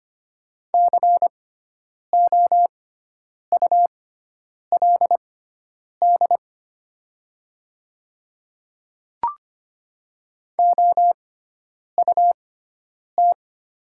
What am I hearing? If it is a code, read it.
Morse code, heard as a keyed tone, about 25 wpm, spaced as for 9 wpm, 700 Hz. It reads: COULD OUT
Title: 6x Character Spacing - Top 200 Words - 25wpm